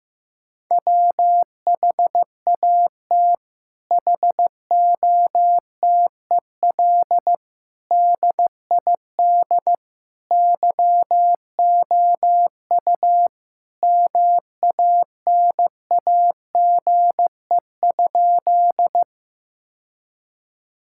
Morse code, heard as a keyed tone, about 15 wpm, 700 Hz. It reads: WHAT HOTEL DID YOU MANAGE?
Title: Sentences from Top 1600 Words 15wpm